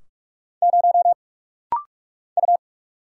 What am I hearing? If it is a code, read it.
Morse code, heard as a keyed tone, about 46 wpm, 700 Hz. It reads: 0 U
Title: Single Letter-Number Alt - Rapid-Fire 45wpm